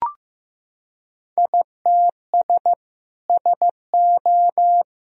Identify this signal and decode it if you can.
Morse code, heard as a keyed tone, about 15 wpm, 700 Hz. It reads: ITS SO
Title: Sentences from Top 1000 Words - Review Part 3 - 15wpm